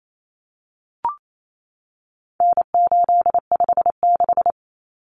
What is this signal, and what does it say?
Morse code, heard as a keyed tone, about 28 wpm, 700 Hz. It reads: N856
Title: Common Cut Numbers - 4-Digit 28wpm